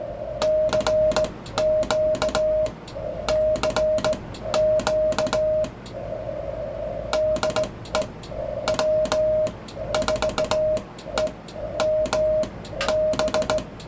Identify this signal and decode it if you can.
Morse code, heard as a keyed tone, about 17 wpm, spaced as for 12 wpm, 620 Hz. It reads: CQ CQ DE W4EMB